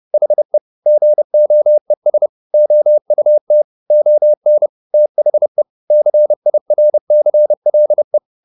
Morse code, heard as a keyed tone, about 30 wpm, 600 Hz. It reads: HE GOES OUT ON THE CIRCLE